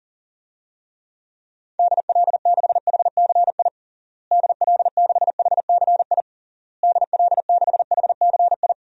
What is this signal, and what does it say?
Morse code, heard as a keyed tone, about 40 wpm, 700 Hz. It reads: DL6HCI DL6HCI DL6HCI
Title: DXCC-100 Call Signs - 6 Characters - Repeated 3x - 40wpm